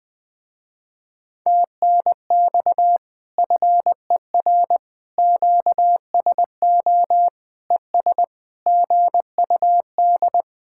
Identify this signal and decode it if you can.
Morse code, heard as a keyed tone, about 20 wpm, 700 Hz. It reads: TNX FER QSO ES GUD
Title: Speed-Racing Common QSO Phrases - 17wpm